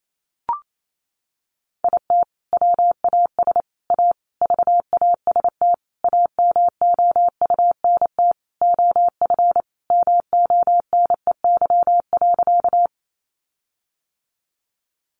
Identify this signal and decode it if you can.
Morse code, heard as a keyed tone, about 28 wpm, 700 Hz. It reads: IT WAS A VAST AMOUNT OF MONEY.
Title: Sentences from Top 1300 Words 28wpm